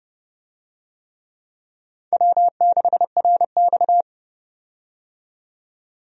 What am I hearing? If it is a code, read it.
Morse code, heard as a keyed tone, about 30 wpm, 700 Hz. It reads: W6RX
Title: USA Call Signs - 4 Characters - 30wpm